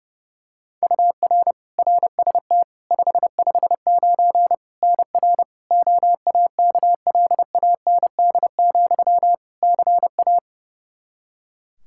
Morse code, heard as a keyed tone, about 30 wpm, 700 Hz. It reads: UR RST 559 NR OAKLAND, CA
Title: K1USN SST QSO Practice 30wpm